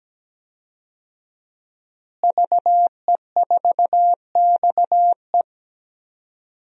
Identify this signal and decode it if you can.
Morse code, heard as a keyed tone, about 17 wpm, 700 Hz. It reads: VE4XE
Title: DXCC-100 Call Signs - 5 Characters - Repeated 2x - 17wpm